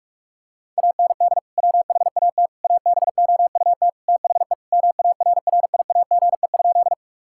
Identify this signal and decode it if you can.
Morse code, heard as a keyed tone, about 41 wpm, 700 Hz. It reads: AND WHAT ABOUT THE MARRIAGE?
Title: Sentences from Top 1000 Words - Review Part 5 - 45wpm